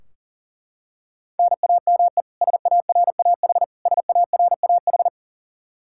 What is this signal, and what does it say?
Morse code, heard as a keyed tone, about 40 wpm, 700 Hz. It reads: NAME SARAH SARAH
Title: Practice QSO - Exchange 1 - Beginning - 40wpm